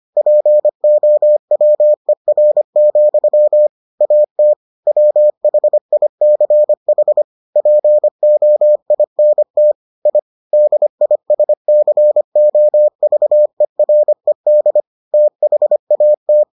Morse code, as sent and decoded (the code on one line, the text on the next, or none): .--. --- .-- . .-. --..-- / .- - / .-- .... .. -.-. .... / .--. --- .. -. - / .. / -.. .. ... -.-. --- ...- . .-. -.. / - .... .- -
POWER, AT WHICH POINT I DISCOVERED THAT